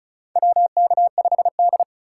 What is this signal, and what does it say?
Morse code, heard as a keyed tone, about 36 wpm, 700 Hz. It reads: WK5D
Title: USA Call Signs - 4 Characters - 3x Character Spacing - 35wpm